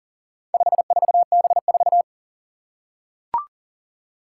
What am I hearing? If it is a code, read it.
Morse code, heard as a keyed tone, about 40 wpm, 700 Hz. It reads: 54B4
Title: Four Challenging Letter-Number Combinations 40wpm